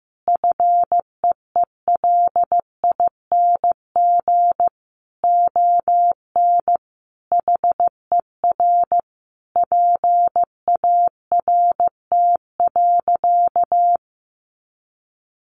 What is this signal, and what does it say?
Morse code, heard as a keyed tone, about 15 wpm, 700 Hz. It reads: FEELING ON HER PART.
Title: Sentences from Top 1200 Words 15wpm